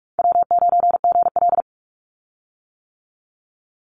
Morse code, heard as a keyed tone, about 46 wpm, 700 Hz. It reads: W9GL
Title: Call Signs - 45wpm